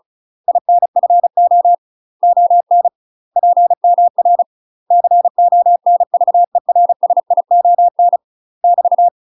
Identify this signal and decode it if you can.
Morse code, heard as a keyed tone, about 34 wpm, 700 Hz. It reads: INFO ON PMR CONVERSION <BT>
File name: UK MASTER QSOs TRAINING / 35WPM / UK MASTER QSOs 0042